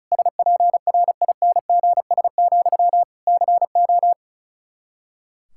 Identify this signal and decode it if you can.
Morse code, heard as a keyed tone, about 36 wpm, 700 Hz. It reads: SPRINGS, CO
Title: Practice QSO - Exchange 1 - Beginning - 35wpm